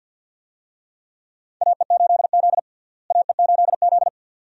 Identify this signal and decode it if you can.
Morse code, heard as a keyed tone, about 52 wpm, 700 Hz. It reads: AE8Z AE8Z
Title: USA Call Signs - 4 Characters - Repeated 2x - 50wpm